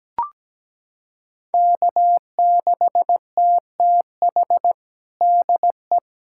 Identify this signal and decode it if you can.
Morse code, heard as a keyed tone, about 17 wpm, 700 Hz. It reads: K6TTH DE
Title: USA Callsign DE Callsign - 5 Characters - 17wpm